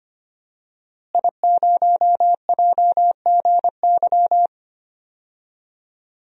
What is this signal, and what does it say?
Morse code, heard as a keyed tone, about 25 wpm, 700 Hz. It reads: I0JGY
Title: DXCC-100 Call Signs - 5 Characters - Repeated 2x - 25wpm